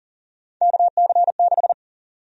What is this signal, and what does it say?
Morse code, heard as a keyed tone, about 40 wpm, 700 Hz. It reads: KC6
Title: USA Call Sign Prefixes - 40wpm